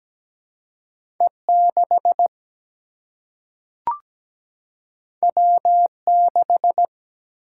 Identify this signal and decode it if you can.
Morse code, heard as a keyed tone, about 17 wpm, 700 Hz. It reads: E 6 W 6
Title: Single-Letter Single-Number 17wpm